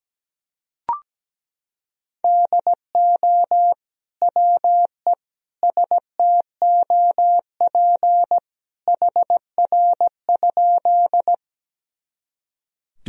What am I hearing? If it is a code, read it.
Morse code, heard as a keyed tone, about 17 wpm, 700 Hz. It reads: DO WE STOP HR?
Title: Sentences from Top 300 Words and Abbreviations 17wpm